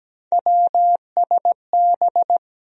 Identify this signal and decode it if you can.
Morse code, heard as a keyed tone, about 17 wpm, 700 Hz. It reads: WSB
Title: USA Call Sign Postfixes - 3 Letters 17wpm